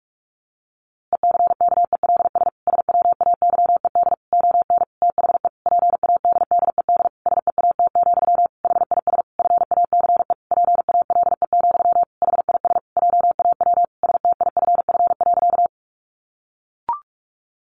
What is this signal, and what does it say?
Morse code, heard as a keyed tone, about 44 wpm, 700 Hz. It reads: ECKELS SWAYED ON THE PADDED SEAT, HIS FACE PALE, HIS JAW STIFF.
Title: A Sound Of Thunder 45wpm